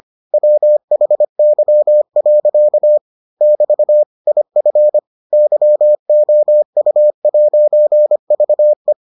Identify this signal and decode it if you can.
Morse code, heard as a keyed tone, about 25 wpm, 600 Hz. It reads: WHY. <BT> IF YOU'VE